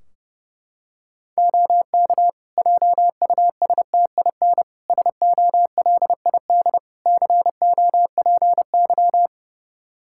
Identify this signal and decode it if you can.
Morse code, heard as a keyed tone, about 30 wpm, 700 Hz. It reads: OK JUSTIN SOLID COPY
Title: Practice QSO - Exchange 3 - More Info - 30wpm